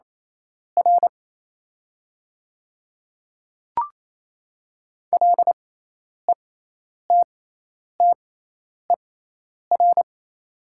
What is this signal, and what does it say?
Morse code, heard as a keyed tone, about 27 wpm, spaced as for 10 wpm, 700 Hz. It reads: R LETTER
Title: 6x Character Spacing - Top 300 Words - 28wpm